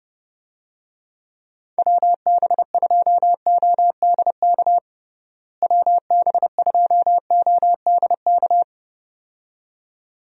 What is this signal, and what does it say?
Morse code, heard as a keyed tone, about 30 wpm, 700 Hz. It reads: WB2ODK WB2ODK
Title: USA Call Signs - 6 Characters - Repeated 2x - 30wpm